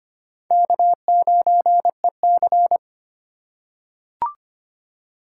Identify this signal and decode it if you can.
Morse code, heard as a keyed tone, about 25 wpm, 700 Hz. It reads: K9EC
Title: USA Call Signs - 4 Characters - 4x Character Spacing - 25wpm